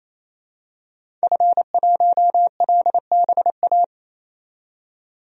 Morse code, heard as a keyed tone, about 28 wpm, 700 Hz. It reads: F1LBA
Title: DXCC-100 Call Signs - 5 Characters - 28wpm